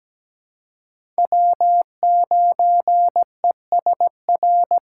Morse code, heard as a keyed tone, about 17 wpm, 700 Hz. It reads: W9ESR
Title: USA Call Signs - 5 Characters - 2x Character Spacing - 17wpm